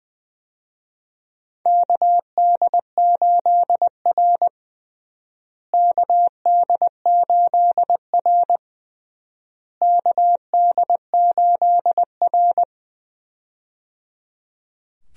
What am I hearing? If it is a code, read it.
Morse code, heard as a keyed tone, about 20 wpm, 700 Hz. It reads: KD8R KD8R KD8R
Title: USA Call Signs - 4 Characters - Repeated 3x - 20wpm